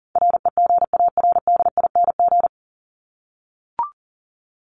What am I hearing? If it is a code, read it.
Morse code, heard as a keyed tone, about 40 wpm, 700 Hz. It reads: REGARDING
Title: ING Suffix - Top 100 Words with Base Words 40wpm